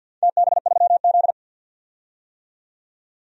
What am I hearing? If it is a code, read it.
Morse code, heard as a keyed tone, about 52 wpm, 700 Hz. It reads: TB3Z
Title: DXCC-100 Call Signs - 4 Characters - 50wpm